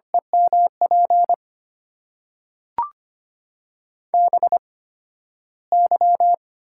Morse code, heard as a keyed tone, about 25 wpm, spaced as for 13 wpm, 700 Hz. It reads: EMP BY
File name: Top 90 QSO Elements - 8x Character Spacing - 25wpm